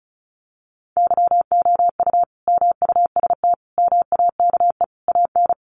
Morse code, heard as a keyed tone, about 34 wpm, 700 Hz. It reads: YOU MUST MAKE AN
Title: Sentences from Top 1100 Words 35wpm